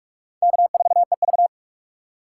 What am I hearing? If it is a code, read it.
Morse code, heard as a keyed tone, about 47 wpm, 700 Hz. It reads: K4EV
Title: USA Call Signs - 4 Characters - 6x Character Spacing - 45wpm